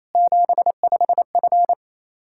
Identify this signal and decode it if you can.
Morse code, heard as a keyed tone, about 28 wpm, 700 Hz. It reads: 75F
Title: Practice QSO - Exchange 2 - Info - 28wpm